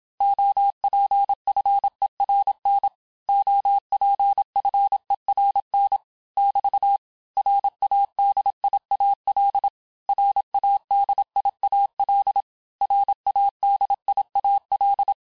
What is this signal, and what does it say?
Morse code, heard as a keyed tone, about 27 wpm, 800 Hz. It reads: OPFERN OPFERN <BT> RADIAL RADIAL RADIAL
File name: cours-du-mercredi / audios / cours-869-texte-LES-MOTS-vitesse-25-MPM